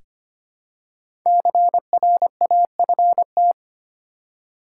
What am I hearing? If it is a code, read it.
Morse code, heard as a keyed tone, about 25 wpm, 700 Hz. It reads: CRAFT
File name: 5 Letter Words - First 3 Letters Given - 25wpm